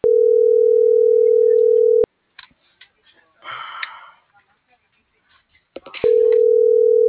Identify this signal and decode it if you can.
Ringback tone (North America), 440 + 480 Hz.
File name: CENTRAL GOMSA / out cisneros.hector